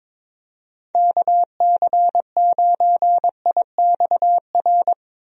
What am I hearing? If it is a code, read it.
Morse code, heard as a keyed tone, about 22 wpm, 700 Hz. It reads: KC9IXR
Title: USA Call Signs - 6 Characters - Repeated 3x - 22wpm